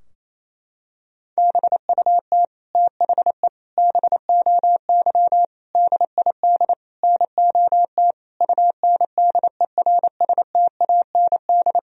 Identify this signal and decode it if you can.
Morse code, heard as a keyed tone, about 28 wpm, 700 Hz. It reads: BUT THE BOY DID NOT UNDERSTAND
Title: Sentences from Top 1000 Words - Review Part 3 - 28wpm